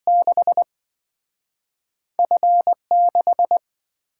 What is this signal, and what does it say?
Morse code, heard as a keyed tone, about 27 wpm, spaced as for 5 wpm, 700 Hz. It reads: 6F 6